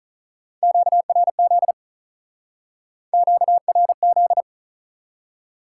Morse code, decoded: QRZ QRZ

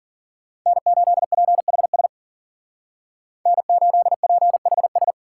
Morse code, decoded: N8PHS N8PHS

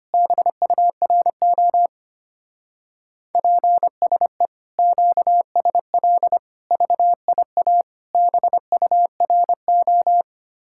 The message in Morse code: BURO PSE QSL VIA BURO

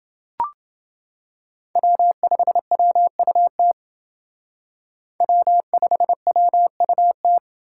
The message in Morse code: W5WUT W5WUT

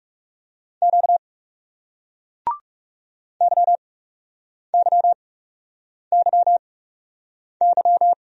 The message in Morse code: Q YYYY

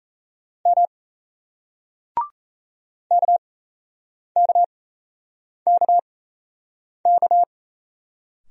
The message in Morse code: M KKKK

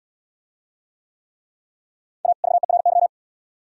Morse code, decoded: A VERY

